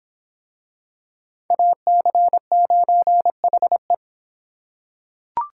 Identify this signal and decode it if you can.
Morse code, heard as a keyed tone, about 25 wpm, 700 Hz. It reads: AC9HE